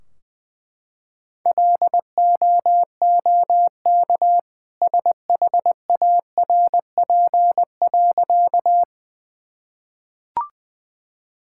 Morse code, heard as a keyed tone, about 20 wpm, 700 Hz. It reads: LOOK SHARP.
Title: Sentences from Top 1600 Words 20wpm